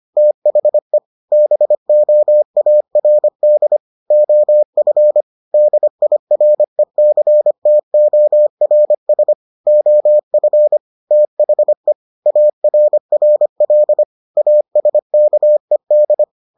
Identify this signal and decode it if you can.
Morse code, heard as a keyed tone, about 25 wpm, 600 Hz. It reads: THE BOARD OF DIRECTORS OF THE ARRL ASKED